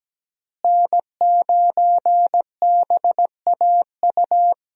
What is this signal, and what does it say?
Morse code, heard as a keyed tone, about 17 wpm, 700 Hz. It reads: N9BAU
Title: USA Call Signs - 5 Characters - Repeated 3x - 17wpm